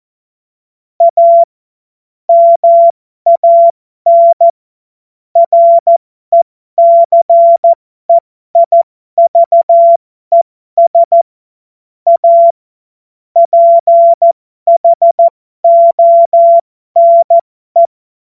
Morse code, heard as a keyed tone, about 14 wpm, spaced as for 12 wpm, 680 Hz. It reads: A MAN RECEIVES A PHONE
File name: Morsekurs / Witze